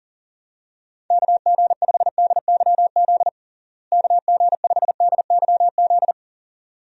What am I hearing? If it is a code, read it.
Morse code, heard as a keyed tone, about 40 wpm, 700 Hz. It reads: KG5DYZ KG5DYZ
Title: USA Call Signs - 6 Characters - Repeated 2x - 40wpm